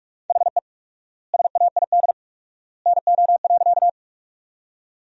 Morse code, heard as a keyed tone, about 46 wpm, 700 Hz. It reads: HE SAID NO.
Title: 4x Word-Spacing - Sentences from Top 100 Words - 45wpm